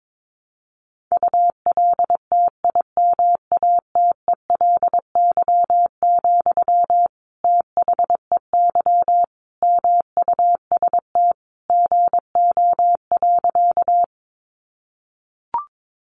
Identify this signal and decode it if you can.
Morse code, heard as a keyed tone, about 22 wpm, 700 Hz. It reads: ULTIMATELY, THEY MUST GO.
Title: Sentences from Top 1200 Words 22wpm